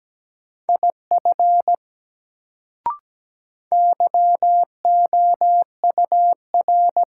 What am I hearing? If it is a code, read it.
Morse code, heard as a keyed tone, about 17 wpm, 700 Hz. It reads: IF YOUR
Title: Sentences from Top 100 Words - New Word Drill - 17wpm